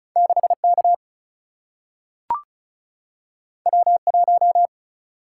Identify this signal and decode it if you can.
Morse code, heard as a keyed tone, about 36 wpm, 700 Hz. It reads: 6K W1
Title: Single-Letter Single-Number 35wpm